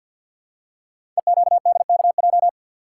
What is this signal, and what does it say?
Morse code, heard as a keyed tone, about 48 wpm, 700 Hz. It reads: EQDKJ
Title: Code Groups - 5 Letters - 50wpm